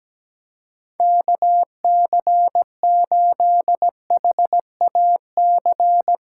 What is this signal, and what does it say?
Morse code, heard as a keyed tone, about 17 wpm, 700 Hz. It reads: KC8HAC
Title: USA Callsign DE Callsign - 6 Characters - 17wpm